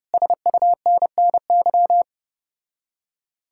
Morse code, heard as a keyed tone, about 33 wpm, 700 Hz. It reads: SUNNY